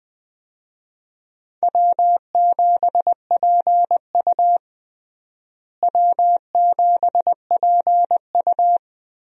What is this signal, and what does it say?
Morse code, heard as a keyed tone, about 20 wpm, 700 Hz. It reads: W7PU W7PU
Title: USA Call Signs - 4 Characters - Repeated 2x - 20wpm